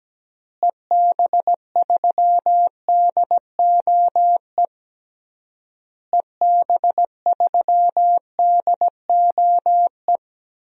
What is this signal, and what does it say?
Morse code, heard as a keyed tone, about 17 wpm, 700 Hz. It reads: EB3DOE EB3DOE
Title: DXCC-100 Call Signs - 6 Characters - Repeated 3x - 17wpm